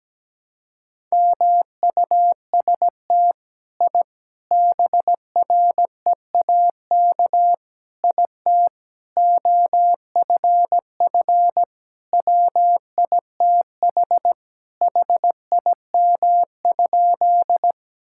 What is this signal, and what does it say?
Morse code, heard as a keyed tone, about 17 wpm, 700 Hz. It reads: MUST I BREAK IT OFF WITH HIM?
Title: Sentences from Top 1000 Words - Review Part 6 - 17wpm